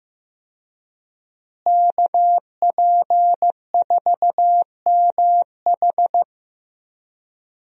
Morse code, heard as a keyed tone, about 15 wpm, 700 Hz. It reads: KP4MH
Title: DXCC-100 Call Signs - 5 Characters - Repeated 2x - 15wpm